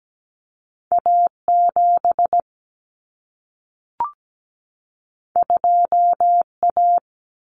Morse code, heard as a keyed tone, about 17 wpm, 700 Hz. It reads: A7 2A
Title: Common Cut Numbers - 2-Digit 17wpm